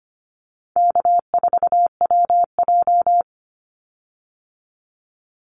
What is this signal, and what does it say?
Morse code, heard as a keyed tone, about 25 wpm, 700 Hz. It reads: K4WJ
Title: Call Signs Repeated 2x - 25wpm